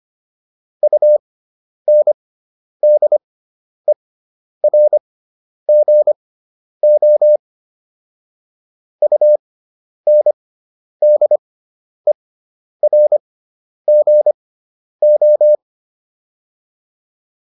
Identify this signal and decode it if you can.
Morse code, heard as a keyed tone, about 25 wpm, spaced as for 10 wpm, 600 Hz. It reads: UNDERGO UNDERGO